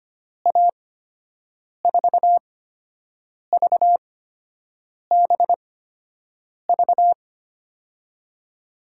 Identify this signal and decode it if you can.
Morse code, heard as a keyed tone, about 25 wpm, spaced as for 7 wpm, 700 Hz. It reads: A4VBV
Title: USA Call Signs - 6 Characters - 8x Character Spacing - 25wpm